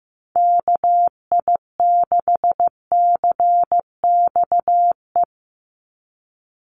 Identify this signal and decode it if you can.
Morse code, heard as a keyed tone, about 15 wpm, 700 Hz. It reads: KI6CXE
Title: USA Call Signs 15wpm